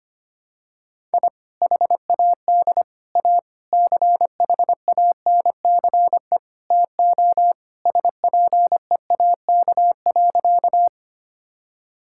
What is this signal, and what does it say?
Morse code, heard as a keyed tone, about 25 wpm, 700 Hz. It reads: I HAD A CHANCE TO SPEAK.
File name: practice / 25wpm / Sentences with 4-Word Phrases - 25wpm